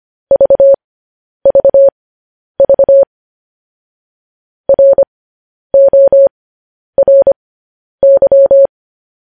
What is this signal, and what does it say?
Morse code, heard as a keyed tone, about 25 wpm, spaced as for 10 wpm, 550 Hz. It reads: VVV RORY
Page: News Headlines in Morse Code at 10 WPM